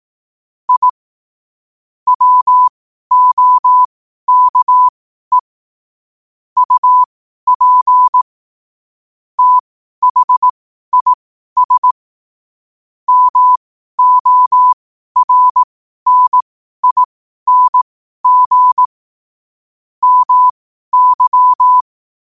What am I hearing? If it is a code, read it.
Morse code, heard as a keyed tone, about 18 wpm, spaced as for 12 wpm, 1000 Hz. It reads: I WOKE UP THIS MORNING MY
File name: qotdfiles / 12wpm